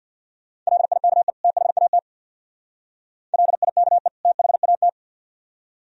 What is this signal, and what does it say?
Morse code, heard as a keyed tone, about 59 wpm, 700 Hz. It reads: LIKE THAT LIKE THAT
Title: Speed-Racing - Sets of 2 Words 50wpm